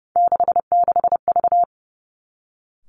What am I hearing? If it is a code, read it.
Morse code, heard as a keyed tone, about 30 wpm, 700 Hz. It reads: 66V
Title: Three Challenging Letter-Number Combinations 30wpm